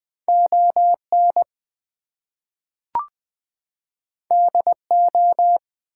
Morse code, heard as a keyed tone, about 20 wpm, 700 Hz. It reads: ON DO
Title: Two Letter Words - 20wpm